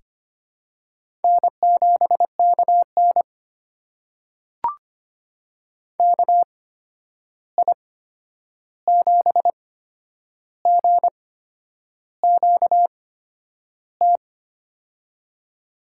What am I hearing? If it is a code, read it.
Morse code, heard as a keyed tone, about 25 wpm, spaced as for 9 wpm, 700 Hz. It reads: N7KN KI7GQT
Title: USA Call Signs - 8x Character Spacing - 25wpm